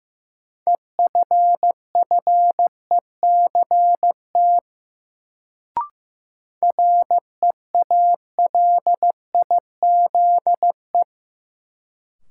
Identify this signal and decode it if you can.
Morse code, heard as a keyed tone, about 15 wpm, 700 Hz. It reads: EFFECT REALIZE
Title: Sentences from Top 1000 Words - New Word Drill - 15wpm